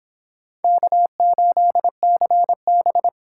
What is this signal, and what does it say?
Morse code, heard as a keyed tone, about 26 wpm, 700 Hz. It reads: K8CB
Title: Speed Racing - USA 1x2 Call Signs 22wpm